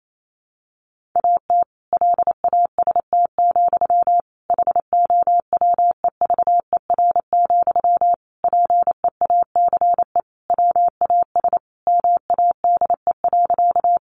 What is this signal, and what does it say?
Morse code, heard as a keyed tone, about 28 wpm, 700 Hz. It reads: AT LAST, HOWEVER, PEACE WAS MADE.
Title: Sentences from Top 1500 Words 28wpm